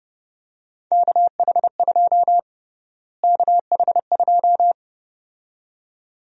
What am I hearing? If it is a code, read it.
Morse code, heard as a keyed tone, about 30 wpm, 700 Hz. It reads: KH2 KH2
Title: USA Call Sign Prefixes - Repeated 2x - 30wpm